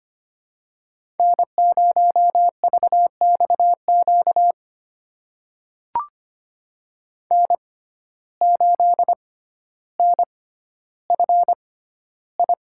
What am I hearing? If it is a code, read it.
Morse code, heard as a keyed tone, about 25 wpm, spaced as for 13 wpm, 700 Hz. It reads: N0VXQ N8NFI
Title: USA Call Signs - 5 Characters - 6x Character Spacing - 25wpm